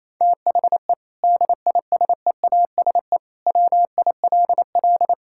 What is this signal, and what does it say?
Morse code, heard as a keyed tone, about 28 wpm, 700 Hz. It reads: THE DISEASE WILL
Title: Sentences from Top 1000 Words - Review Part 6 - 28wpm